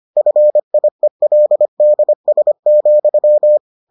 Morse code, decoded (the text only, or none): FIELDS,